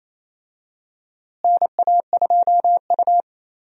Morse code, heard as a keyed tone, about 28 wpm, 700 Hz. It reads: NA2U
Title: USA Call Signs - 4 Characters - Repeated 2x - 28wpm